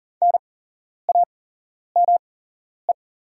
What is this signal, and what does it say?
Morse code, heard as a keyed tone, about 40 wpm, spaced as for 11 wpm, 700 Hz. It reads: NAME
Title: 8x Character Spacing - Top 200 Words - 40wpm